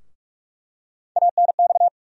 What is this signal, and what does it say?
Morse code, heard as a keyed tone, about 46 wpm, 700 Hz. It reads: ANX
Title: USA Call Sign Postfixes - 3 Letters 45wpm